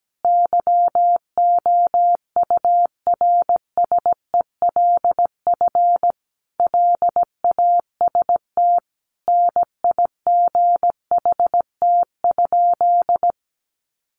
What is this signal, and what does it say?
Morse code, heard as a keyed tone, about 17 wpm, 700 Hz. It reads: YOURSELF LAST NIGHT?
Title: Sentences from Top 600 Words 17wpm